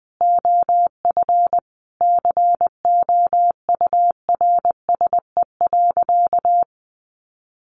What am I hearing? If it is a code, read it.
Morse code, heard as a keyed tone, about 19 wpm, 700 Hz. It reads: OF COURSE.
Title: Sentences from Top 900 Words 20wpm